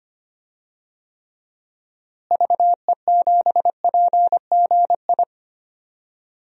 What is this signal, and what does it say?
Morse code, heard as a keyed tone, about 25 wpm, 700 Hz. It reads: VE7PGI
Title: DXCC-100 Call Signs - 6 Characters - Repeated 2x - 25wpm